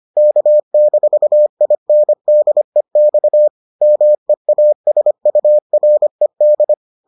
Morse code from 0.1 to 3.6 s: K-INDEX

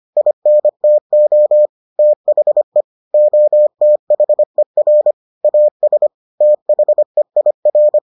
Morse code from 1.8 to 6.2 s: THE OTHER AS

